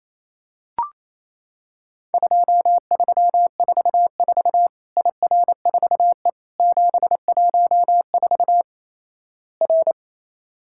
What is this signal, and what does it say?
Morse code, heard as a keyed tone, about 28 wpm, 700 Hz. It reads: 2344 IR4E 714 R
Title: Worked All Europe DX Contest - QTC Exchange - 28wpm